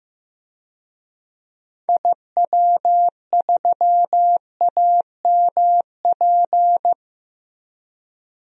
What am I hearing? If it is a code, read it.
Morse code, heard as a keyed tone, about 15 wpm, 700 Hz. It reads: IW3AMP
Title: DXCC-100 Call Signs - 6 Characters - Repeated 3x - 15wpm